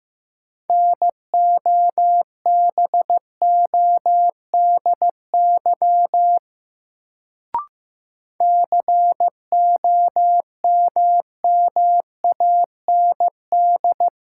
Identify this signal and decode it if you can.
Morse code, heard as a keyed tone, about 15 wpm, 700 Hz. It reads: NOBODY COMMAND
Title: Sentences from Top 1600 Words - New Word Drill - 15wpm